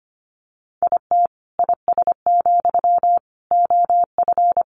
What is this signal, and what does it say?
Morse code, heard as a keyed tone, about 25 wpm, 700 Hz. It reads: IT IS, OF
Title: Sentences from Top 1200 Words 25wpm